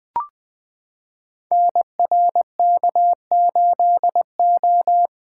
Morse code, heard as a keyed tone, about 20 wpm, 700 Hz. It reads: NRK8O
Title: Code Groups - 5 Letter:Numbers - 20wpm